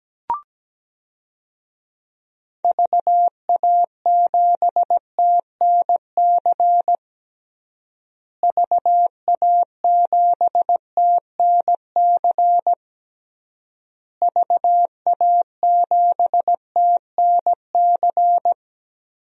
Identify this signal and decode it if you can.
Morse code, heard as a keyed tone, about 17 wpm, 700 Hz. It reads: VA7TNC VA7TNC VA7TNC
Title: DXCC-100 Call Signs - 6 Characters - Repeated 3x - 17wpm